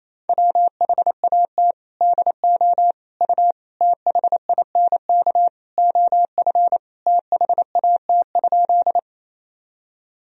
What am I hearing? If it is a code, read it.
Morse code, heard as a keyed tone, about 28 wpm, 700 Hz. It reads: WHAT DO U THINK OF THAT?
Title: Sentences from Top 200 Words and Abbreviations 28wpm